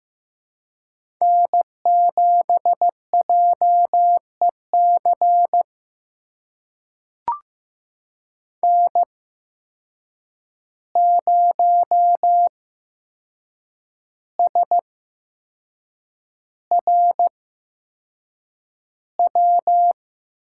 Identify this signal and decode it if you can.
Morse code, heard as a keyed tone, about 15 wpm, spaced as for 6 wpm, 700 Hz. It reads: N7JEC N0SRW
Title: USA Call Signs - 5 Characters - 8x Character Spacing - 15wpm